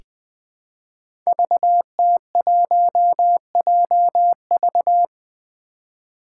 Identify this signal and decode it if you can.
Morse code, heard as a keyed tone, about 20 wpm, 700 Hz. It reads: VT1JV
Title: Five Challenging Letter-Number Combinations 20wpm